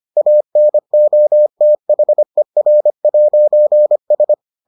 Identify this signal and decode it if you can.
Morse code, heard as a keyed tone, about 25 wpm, 600 Hz. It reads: ANOTHER'S